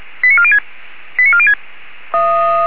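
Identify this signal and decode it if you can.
Five-tone selective call (ZVEI), heard as quick runs of five tones, tones of about 70 ms. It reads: 89406 89406
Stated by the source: Five-tone ZVEI selective call received by SDR.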